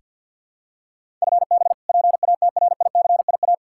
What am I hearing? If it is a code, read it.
Morse code, heard as a keyed tone, about 45 wpm, 700 Hz. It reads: FB PATRICIA
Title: Practice QSO - Exchange 3 - More Info - 50wpm